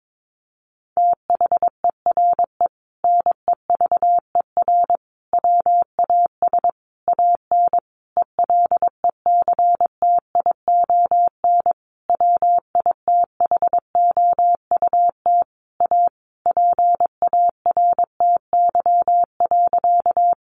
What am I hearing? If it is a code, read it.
Morse code, heard as a keyed tone, about 22 wpm, 700 Hz. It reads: THERE NEVER WAS AN ELECTION WITHOUT A PARTY.
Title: Sentences from Top 1500 Words 22wpm